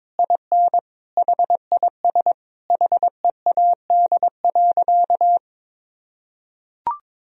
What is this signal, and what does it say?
Morse code, heard as a keyed tone, about 23 wpm, 700 Hz. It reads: IN HIS HEAD.
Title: Sentences from Top 1000 Words - Review Part 3 - 22wpm